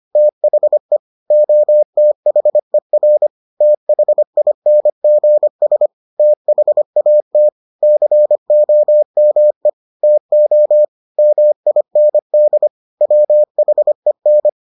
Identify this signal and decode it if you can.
Morse code, heard as a keyed tone, about 25 wpm, 600 Hz. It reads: THE OTHER THINGS THAT COME TO MIND WHEN